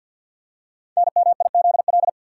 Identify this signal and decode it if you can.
Morse code, heard as a keyed tone, about 52 wpm, 700 Hz. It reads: NMIZL
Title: Code Groups - 5 Letters - 50wpm